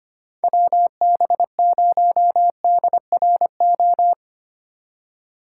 Morse code, heard as a keyed tone, about 25 wpm, 700 Hz. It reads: WB0DRO
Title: USA Call Signs - 6 Characters - 3x Character Spacing - 25wpm